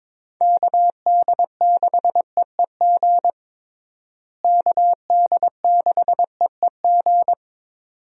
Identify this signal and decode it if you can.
Morse code, heard as a keyed tone, about 22 wpm, 700 Hz. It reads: KD6EEG KD6EEG
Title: USA Call Signs - 6 Characters - Repeated 2x - 22wpm